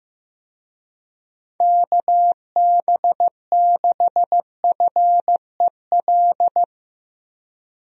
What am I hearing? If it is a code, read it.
Morse code, heard as a keyed tone, about 15 wpm, 700 Hz. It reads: KB6FEL
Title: USA Call Signs - 6 Characters - Repeated 2x - 15wpm